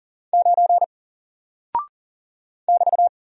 Morse code, heard as a keyed tone, about 39 wpm, 700 Hz. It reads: 9 <BT>
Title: Single Character (Letter:Number:Punctuation:Prosign) 40wpm